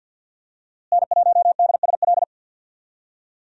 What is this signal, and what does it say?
Morse code, heard as a keyed tone, about 50 wpm, 700 Hz. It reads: N1DSL
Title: USA Call Signs - 5 Characters - Repeated 2x - 50wpm